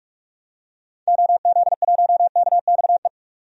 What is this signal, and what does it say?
Morse code, heard as a keyed tone, about 47 wpm, 700 Hz. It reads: OZ1KXE
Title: DXCC-100 Call Signs - 6 Characters - 45wpm